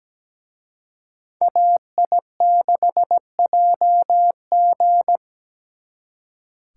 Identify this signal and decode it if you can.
Morse code, heard as a keyed tone, about 20 wpm, 700 Hz. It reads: AI6JG
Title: Speed Racing - USA 2x2 Call Signs 17wpm